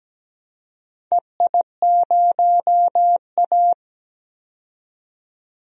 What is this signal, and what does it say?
Morse code, heard as a keyed tone, about 17 wpm, 700 Hz. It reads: EI0A
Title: DXCC-100 Call Signs - 4 Characters - 17wpm